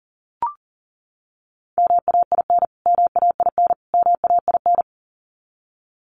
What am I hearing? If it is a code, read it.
Morse code, heard as a keyed tone, about 40 wpm, 700 Hz. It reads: MAIN MAIN MAIN
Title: Top 300 Words Repeated 40wpm